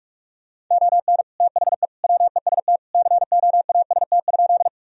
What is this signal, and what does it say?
Morse code, heard as a keyed tone, about 42 wpm, 700 Hz. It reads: ON THE WEST COAST?
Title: Sentences from Top 1000 Words - Review Part 4 - 45wpm